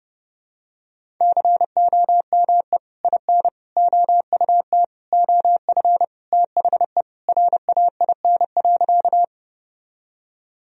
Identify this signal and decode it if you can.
Morse code, heard as a keyed tone, about 30 wpm, 700 Hz. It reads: COME IN OUT OF THE RAIN.
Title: Sentences from Top 1000 Words - Review Part 1 - 30wpm